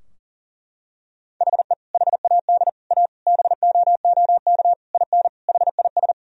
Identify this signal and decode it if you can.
Morse code, heard as a keyed tone, about 40 wpm, 700 Hz. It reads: HE HAD A BOOK IN HIS
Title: Sentences from Top 1000 Words - Review Part 3 - 40wpm